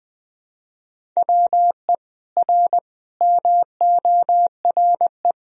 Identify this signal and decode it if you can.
Morse code, heard as a keyed tone, about 20 wpm, 700 Hz. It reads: WE R MORE
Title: Sentences from Top 500 Words and Abbreviations 20wpm